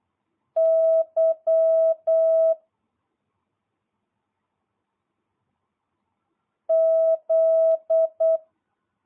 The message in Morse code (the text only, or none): Y Z